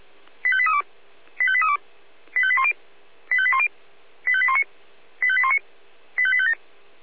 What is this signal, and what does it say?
Five-tone selective call (ZVEI), heard as quick runs of five tones, tones of about 70 ms. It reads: 85832 85832 85810 85810 85819 85819 85857
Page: /496/SDR/ZVEI/85855/